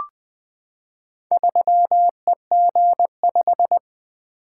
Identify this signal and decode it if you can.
Morse code, heard as a keyed tone, about 20 wpm, 700 Hz. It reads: 3EG5
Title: Four Letter-Number Combinations - 20wpm